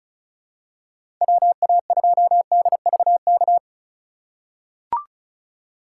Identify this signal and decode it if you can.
Morse code, heard as a keyed tone, about 36 wpm, 690 Hz. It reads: WA2DVK
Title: USA Call Signs - 6 Characters - 6x Character Spacing - 35wpm